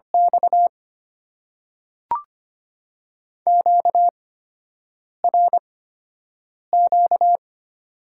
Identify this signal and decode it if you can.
Morse code, heard as a keyed tone, about 25 wpm, spaced as for 7 wpm, 700 Hz. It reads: X QRQ
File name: Top 90 QSO Elements - 8x Character Spacing - 25wpm